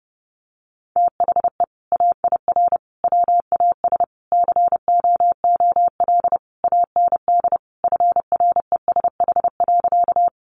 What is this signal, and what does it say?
Morse code, heard as a keyed tone, about 30 wpm, 700 Hz. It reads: THE AIR WAS COOL AND FRESH.
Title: Sentences from Top 1200 Words 30wpm